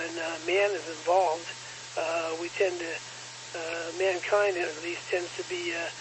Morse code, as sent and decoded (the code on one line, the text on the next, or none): none